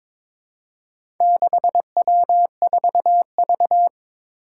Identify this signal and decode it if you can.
Morse code, heard as a keyed tone, about 23 wpm, 700 Hz. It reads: 6W4V
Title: Four Challenging Letter-Number Combinations 22wpm